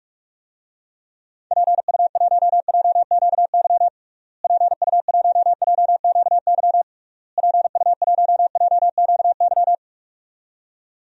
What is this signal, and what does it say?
Morse code, heard as a keyed tone, about 41 wpm, 700 Hz. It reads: PU1JQY PU1JQY PU1JQY
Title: DXCC-100 Call Signs - 6 Characters - Repeated 3x - 45wpm